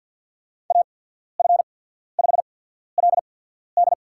Morse code, heard as a keyed tone, about 52 wpm, spaced as for 14 wpm, 700 Hz. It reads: AF5LD
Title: USA Call Signs - 5 Characters - 8x Character Spacing - 50wpm